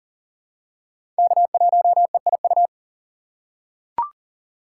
Morse code, heard as a keyed tone, about 40 wpm, 700 Hz. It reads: K1EIU
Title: USA Call Signs - 5 Characters - 4x Character Spacing - 40wpm